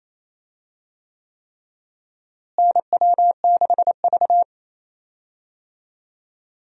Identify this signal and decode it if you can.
Morse code, heard as a keyed tone, about 28 wpm, 700 Hz. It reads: NW6V
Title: USA Call Signs - 4 Characters - Repeated 3x - 28wpm